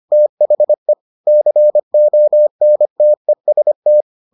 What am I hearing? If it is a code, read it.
Morse code, heard as a keyed tone, about 25 wpm, 600 Hz. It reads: THE CONTEST